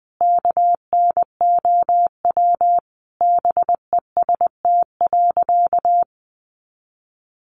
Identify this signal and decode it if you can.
Morse code, heard as a keyed tone, about 20 wpm, 700 Hz. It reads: KNOW BEST.
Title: Sentences from Top 600 Words 20wpm